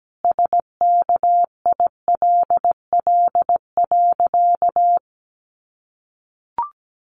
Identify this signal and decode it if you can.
Morse code, heard as a keyed tone, about 17 wpm, 700 Hz. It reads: SKILL.
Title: Sentences from Top 1200 Words 17wpm